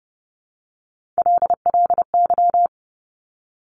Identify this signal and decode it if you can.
Morse code, heard as a keyed tone, about 30 wpm, 690 Hz. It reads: LLY
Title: Top Three Letter Combinations in Words 30wpm